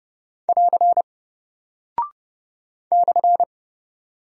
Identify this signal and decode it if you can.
Morse code, heard as a keyed tone, about 30 wpm, 700 Hz. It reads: <AR> /
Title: Single Character (Letter:Number:Punctuation:Prosign) 30wpm